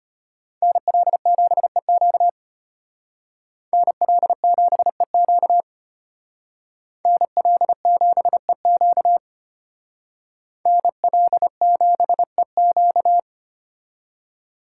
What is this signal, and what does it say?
Morse code, heard as a keyed tone, about 33 wpm, 700 Hz. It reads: NL7EQ NL7EQ NL7EQ NL7EQ